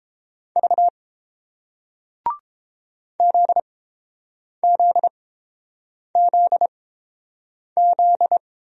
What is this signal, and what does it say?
Morse code, heard as a keyed tone, about 30 wpm, spaced as for 7 wpm, 700 Hz. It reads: V ZZZZ